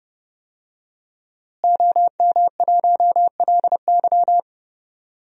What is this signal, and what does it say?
Morse code, heard as a keyed tone, about 30 wpm, 700 Hz. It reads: OM1LY
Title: DXCC-100 Call Signs - 5 Characters - Repeated 3x - 30wpm